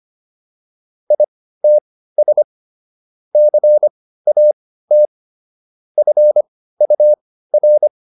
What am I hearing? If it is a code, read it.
Morse code, heard as a keyed tone, about 25 wpm, spaced as for 15 wpm, 600 Hz. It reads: ITS CAT FUR